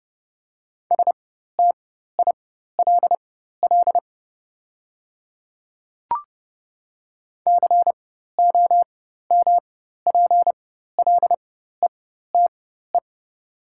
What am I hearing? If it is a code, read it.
Morse code, heard as a keyed tone, about 30 wpm, spaced as for 14 wpm, 700 Hz. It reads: STILL COMPLETE
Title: 4x Character Spacing - Top 500 Words - 30wpm